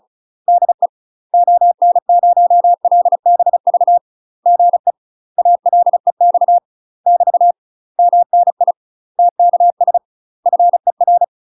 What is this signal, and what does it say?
Morse code, heard as a keyed tone, about 34 wpm, 700 Hz. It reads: DE ON0LBV GE ALEX <BT> MNI TKS FER
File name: UK MASTER QSOs TRAINING / 35WPM / UK MASTER QSOs 0011